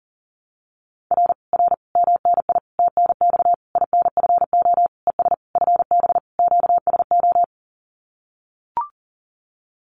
Morse code, heard as a keyed tone, about 38 wpm, 700 Hz. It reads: R R MNI TNX INFO ES FB QSO
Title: Common QSO Phrases Encore 40wpm